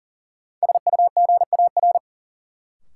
Morse code, heard as a keyed tone, about 40 wpm, 700 Hz. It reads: SUGAR
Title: Sentences from Top 1200 Words - New Word Drill - 40wpm